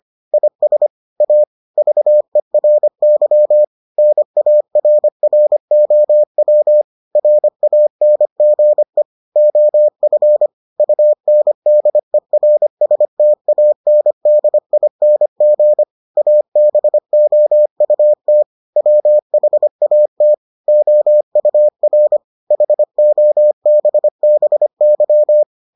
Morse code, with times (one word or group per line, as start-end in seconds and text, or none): 0.3-0.9 IS
1.2-1.4 A
1.8-3.6 VERY
4.0-6.8 NARROW
7.2-9.0 RANGE
9.4-10.5 OF
10.8-15.8 UNDERSTANDING
16.2-18.4 ABOUT
18.8-20.3 WHAT
20.7-22.2 OUR
22.5-25.4 HOBBY